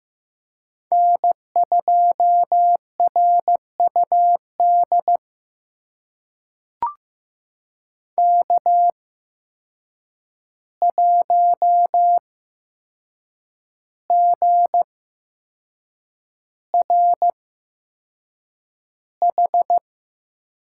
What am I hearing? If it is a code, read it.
Morse code, heard as a keyed tone, about 15 wpm, spaced as for 6 wpm, 700 Hz. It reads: N2RUD K1GRH
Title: USA Call Signs - 8x Character Spacing - 15wpm